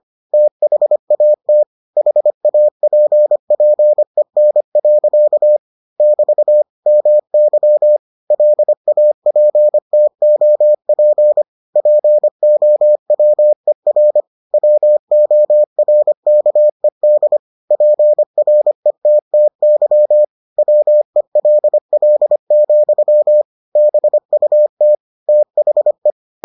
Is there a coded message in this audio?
Yes: Morse code from 0.3 to 26.1 s, heard as a keyed tone, about 25 wpm, 600 Hz. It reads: THAT HAPPEN. <BT> MY LAPTOP POWER WORKED PRETTY WELL, BUT THE